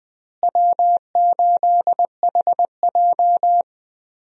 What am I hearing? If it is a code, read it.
Morse code, heard as a keyed tone, about 20 wpm, 700 Hz. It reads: W8HJ
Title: USA Call Signs - 4 Characters - Repeated 2x - 20wpm